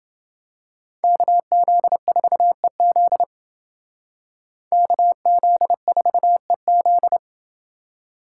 Morse code, heard as a keyed tone, about 29 wpm, 700 Hz. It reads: KZ4EZ KZ4EZ